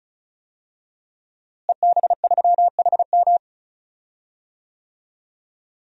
Morse code, heard as a keyed tone, about 36 wpm, 700 Hz. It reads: EB3HM
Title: DXCC-100 Call Signs - 5 Characters - Repeated 2x - 35wpm